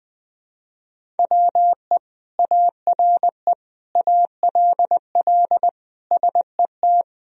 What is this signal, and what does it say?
Morse code, heard as a keyed tone, about 20 wpm, 700 Hz. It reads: WE ARE ALL SET
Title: Sentences from Top 1000 Words - Review Part 1 - 20wpm